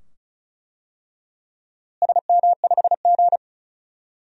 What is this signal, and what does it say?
Morse code, heard as a keyed tone, about 36 wpm, 700 Hz. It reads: SM5G
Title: DXCC-100 Call Signs - 4 Characters - Repeated 2x - 35wpm